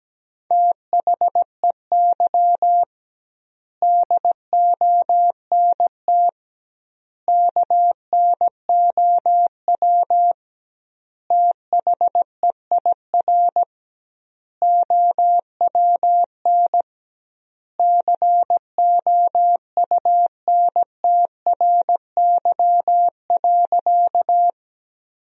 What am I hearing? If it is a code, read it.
Morse code, heard as a keyed tone, about 17 wpm, 700 Hz. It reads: THEY DONT KNOW THEIR OWN COUNTRY.
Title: 2x Word-Spacing - Sentences from Top 300 Words - 17wpm